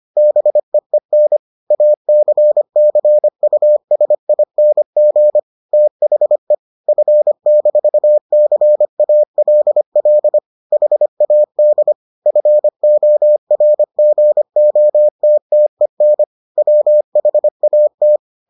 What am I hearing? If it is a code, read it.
Morse code, heard as a keyed tone, about 25 wpm, 600 Hz. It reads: BEEN ACCUSING THE F-CALL HAD FORGOTTEN WHAT